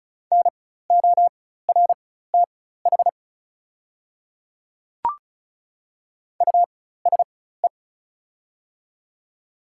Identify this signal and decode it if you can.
Morse code, heard as a keyed tone, about 36 wpm, spaced as for 16 wpm, 690 Hz. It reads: NORTH USE
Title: 4x Character Spacing - Top 500 Words - 35wpm